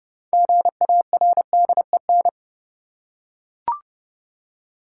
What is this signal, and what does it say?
Morse code, heard as a keyed tone, about 31 wpm, 700 Hz. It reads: GARDEN
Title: Six Letter Words - 30wpm